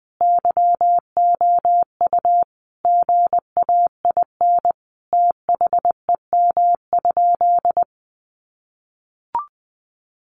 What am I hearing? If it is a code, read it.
Morse code, heard as a keyed tone, about 20 wpm, 700 Hz. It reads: YOU GAIN THEM?
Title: Sentences from Top 1000 Words 20wpm